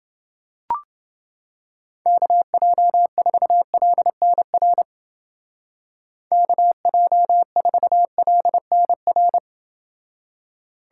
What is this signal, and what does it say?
Morse code, heard as a keyed tone, about 28 wpm, 700 Hz. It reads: KJ4LNR KJ4LNR